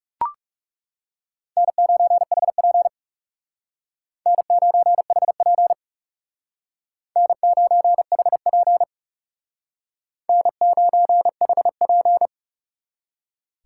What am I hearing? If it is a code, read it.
Morse code, heard as a keyed tone, about 37 wpm, 700 Hz. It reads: N9HP N9HP N9HP N9HP
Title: Speed Racing - USA Call Signs 30wpm